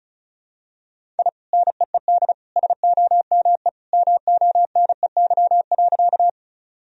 Morse code, heard as a keyed tone, about 34 wpm, 700 Hz. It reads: I NEED SOME MONEY.
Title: Sentences from Top 1000 Words - Review Part 1 - 35wpm